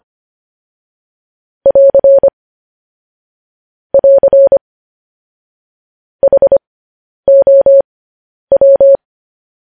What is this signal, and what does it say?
Morse code, heard as a keyed tone, about 25 wpm, spaced as for 10 wpm, 550 Hz. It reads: <AR> <AR> HOW